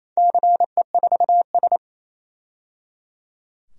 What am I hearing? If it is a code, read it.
Morse code, heard as a keyed tone, about 28 wpm, 700 Hz. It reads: CE4S
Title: DXCC-100 Call Signs - 4 Characters - 28wpm